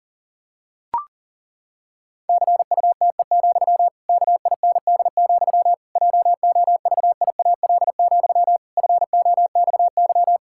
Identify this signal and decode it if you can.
Morse code, heard as a keyed tone, about 37 wpm, 700 Hz. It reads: CUTE, KIND, JOVIAL, FOXY
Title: Pangram Challenge v2 - 40wpm